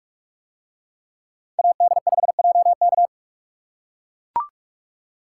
Morse code, heard as a keyed tone, about 42 wpm, 700 Hz. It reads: AD5JK